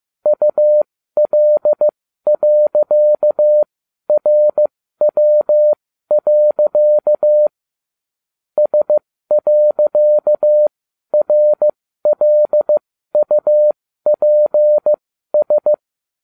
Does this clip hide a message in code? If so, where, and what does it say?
Yes: Morse code from 0.3 to 15.7 s, heard as a keyed tone, about 15 wpm, spaced as for 12 wpm, 600 Hz. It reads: UL.RW. S.RLUPS